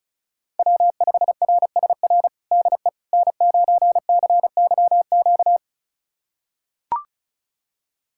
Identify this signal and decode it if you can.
Morse code, heard as a keyed tone, about 36 wpm, 700 Hz. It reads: W5RSR DE N9CYQ
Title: USA Callsign DE Callsign - 5 Characters - 35wpm